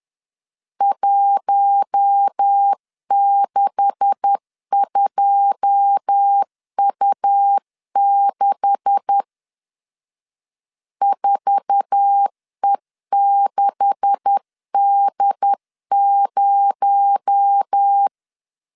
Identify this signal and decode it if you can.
Morse code, heard as a keyed tone, about 10 wpm, 800 Hz. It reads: 162U6 4E6D0
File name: LEZIONI / 9 286 10wpm 2735